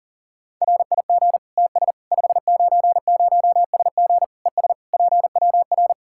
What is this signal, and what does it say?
Morse code, heard as a keyed tone, about 37 wpm, 700 Hz. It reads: RIG TS 590SG ES PWR
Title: Practice QSO - Exchange 2 - Info - 40wpm